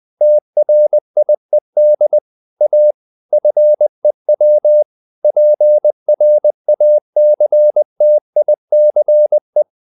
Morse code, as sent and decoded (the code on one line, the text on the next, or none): - .-. .. -.. / .- / ..-. .-- / .--. .-. .- -.-. - .. -.-. .
TRIED A FEW PRACTICE